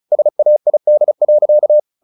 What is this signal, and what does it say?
Morse code, heard as a keyed tone, about 34 wpm, 600 Hz. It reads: SAID.
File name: the old man and the sea 2 35wpm